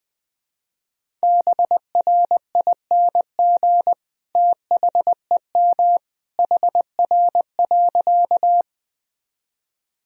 Morse code, heard as a keyed tone, about 20 wpm, 700 Hz. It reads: BRING THEM HR.
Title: Sentences from Top 500 Words and Abbreviations 20wpm